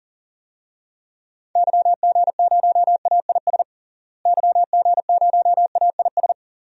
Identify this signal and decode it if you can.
Morse code, heard as a keyed tone, about 40 wpm, 700 Hz. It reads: YG0AIS YG0AIS
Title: DXCC-100 Call Signs - 6 Characters - Repeated 3x - 40wpm